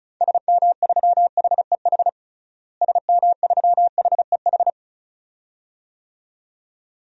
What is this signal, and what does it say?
Morse code, heard as a keyed tone, about 34 wpm, 700 Hz. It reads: SM3HEH SM3HEH
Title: DXCC-100 Call Signs - 6 Characters - Repeated 2x - 35wpm